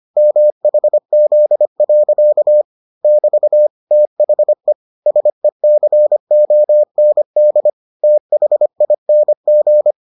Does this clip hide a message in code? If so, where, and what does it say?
Yes: Morse code from 0.2 to 9.9 s, heard as a keyed tone, about 25 wpm, 600 Hz. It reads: MHZ. <BT> THE SECOND THING